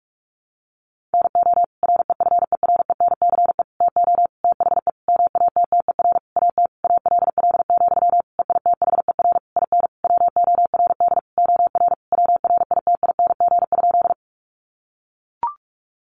Morse code, heard as a keyed tone, about 44 wpm, 700 Hz. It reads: NO REFERENCE TO THE MATTER AT ALL, EITHER IN WORD OR WRITING?
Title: Sentences from Top 1200 Words 45wpm